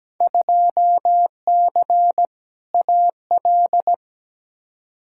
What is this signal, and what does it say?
Morse code, heard as a keyed tone, about 17 wpm, 700 Hz. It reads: 2C AL
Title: Field Day Exchanges Encore - 17wpm